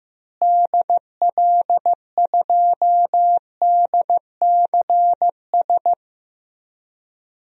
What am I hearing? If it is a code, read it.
Morse code, heard as a keyed tone, about 15 wpm, 700 Hz. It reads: DL2DCS
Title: DXCC-100 Call Signs - 6 Characters - Repeated 3x - 15wpm